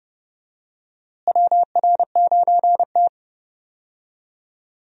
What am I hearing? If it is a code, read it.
Morse code, heard as a keyed tone, about 30 wpm, 700 Hz. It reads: WR9T
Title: USA Call Signs - 4 Characters - Repeated 2x - 30wpm